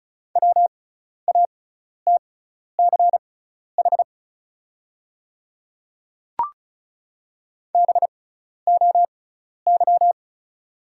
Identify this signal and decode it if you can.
Morse code, heard as a keyed tone, about 36 wpm, spaced as for 12 wpm, 690 Hz. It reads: WATCH BOY
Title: 6x Character Spacing - Top 300 Words - 35wpm